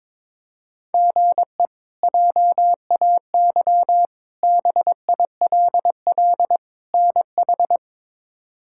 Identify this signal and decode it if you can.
Morse code, heard as a keyed tone, about 22 wpm, 700 Hz. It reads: GE JAY BILL NH
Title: K1USN SST QSO Practice - 22wpm